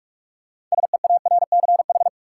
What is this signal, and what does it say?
Morse code, heard as a keyed tone, about 42 wpm, 700 Hz. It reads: SEARCH
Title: Six Letter Words - 45wpm